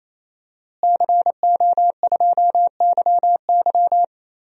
Morse code, heard as a keyed tone, about 28 wpm, 700 Hz. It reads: CO2YY
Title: DXCC-100 Call Signs - 5 Characters - Repeated 3x - 28wpm